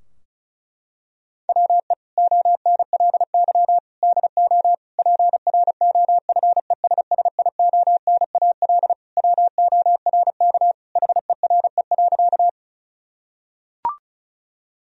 Morse code, heard as a keyed tone, about 36 wpm, 700 Hz. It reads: WE ONLY DO PROFESSIONAL WORK HERE.
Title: Sentences from Top 1000 Words - Review Part 4 - 35wpm